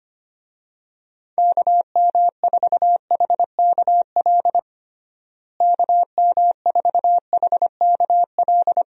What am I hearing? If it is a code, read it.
Morse code, heard as a keyed tone, about 25 wpm, 700 Hz. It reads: KM4HKL KM4HKL
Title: USA Call Signs - 6 Characters - Repeated 2x - 25wpm